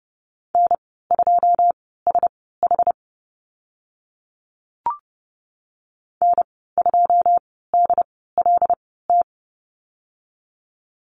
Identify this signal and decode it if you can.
Morse code, heard as a keyed tone, about 30 wpm, spaced as for 17 wpm, 700 Hz. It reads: N2SH N2DLT
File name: Lesson 15g - Number 5 - Call Signs 30wpm